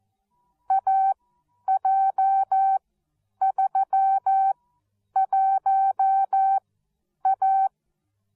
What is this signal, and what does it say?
Morse code, heard as a keyed tone, about 14 wpm, spaced as for 9 wpm, 770 Hz. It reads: AJ31A